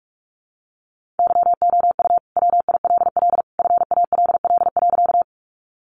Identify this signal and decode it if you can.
Morse code, heard as a keyed tone, about 48 wpm, 700 Hz. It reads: YOU WILL FALL.
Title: Sentences from Top 500 Words 45wpm